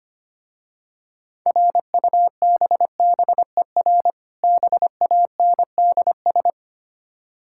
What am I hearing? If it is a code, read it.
Morse code, heard as a keyed tone, about 25 wpm, 700 Hz. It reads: RUBBER BANDS
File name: practice / 25wpm / Difficult Words with Hints - 25wpm